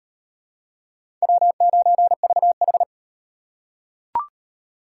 Morse code, heard as a keyed tone, about 38 wpm, 700 Hz. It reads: W9VH